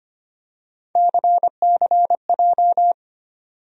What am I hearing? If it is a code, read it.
Morse code, heard as a keyed tone, about 25 wpm, 700 Hz. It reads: CCJ
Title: USA Call Sign Postfixes - 3 Letters 25wpm